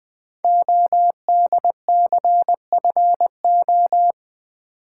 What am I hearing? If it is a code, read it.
Morse code, heard as a keyed tone, about 20 wpm, 700 Hz. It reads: ODCFO
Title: Code Groups - 5 Letters - 20wpm